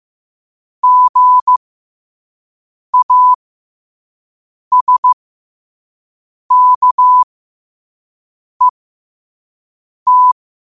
Morse code, heard as a keyed tone, about 15 wpm, spaced as for 5 wpm, 1000 Hz. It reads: GASKET
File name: qotdfiles / 5wpm